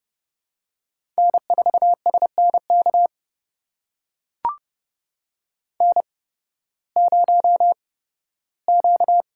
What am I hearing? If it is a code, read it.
Morse code, heard as a keyed tone, about 30 wpm, spaced as for 16 wpm, 700 Hz. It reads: N4SNK N0Q
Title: USA Call Signs - 5 Characters - 8x Character Spacing - 30wpm